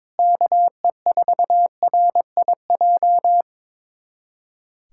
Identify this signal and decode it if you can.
Morse code, heard as a keyed tone, about 22 wpm, 700 Hz. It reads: KE4RIJ
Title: USA Callsign DE Callsign 22wpm